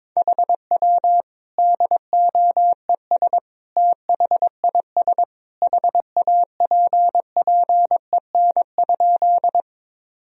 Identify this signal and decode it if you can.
Morse code, heard as a keyed tone, about 22 wpm, 700 Hz. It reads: HW DOES THIS HAPPEN?
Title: Sentences from Top 400 Words and Abbreviations 22wpm